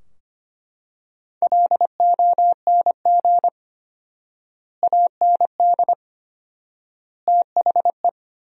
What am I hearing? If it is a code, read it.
Morse code, heard as a keyed tone, about 25 wpm, 700 Hz. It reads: LONG AND THE
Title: 4x Word-Spacing - Sentences from Top 400 Words - 25wpm